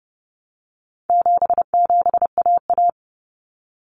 Morse code, heard as a keyed tone, about 30 wpm, 700 Hz. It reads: 77AA
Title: Common Cut Numbers - 4-Digit 30wpm